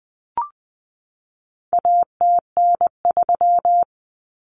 Morse code, heard as a keyed tone, about 20 wpm, 700 Hz. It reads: ATN3
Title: Common Cut Numbers - 4-Digit 20wpm